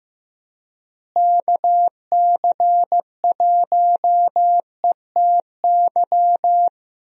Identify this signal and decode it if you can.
Morse code, heard as a keyed tone, about 15 wpm, 700 Hz. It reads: KC1ETY
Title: USA Call Signs - 6 Characters - Repeated 3x - 15wpm